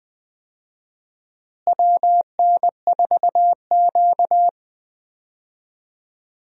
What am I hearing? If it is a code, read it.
Morse code, heard as a keyed tone, about 20 wpm, 700 Hz. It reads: WN4Q
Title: USA Call Signs - 4 Characters - Repeated 3x - 20wpm